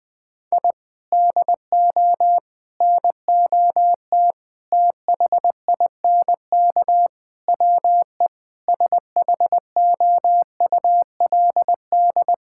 Morse code, heard as a keyed tone, about 20 wpm, 700 Hz. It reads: I DO NOT THINK WE SHOULD
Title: Sentences with 5-Word Phrases - 20wpm